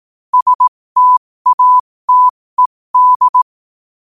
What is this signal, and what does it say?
Morse code, heard as a keyed tone, about 18 wpm, spaced as for 15 wpm, 1000 Hz. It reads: STATED
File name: qotdfiles / 15wpm